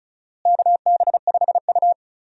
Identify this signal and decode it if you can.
Morse code, heard as a keyed tone, about 36 wpm, 700 Hz. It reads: KB5U
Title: USA Call Signs - 4 Characters - Repeated 3x - 35wpm